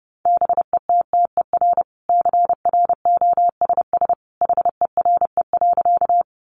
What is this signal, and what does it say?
Morse code, heard as a keyed tone, about 30 wpm, 700 Hz. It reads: BETTER CROSS HERE.
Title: Sentences from Top 400 Words 30wpm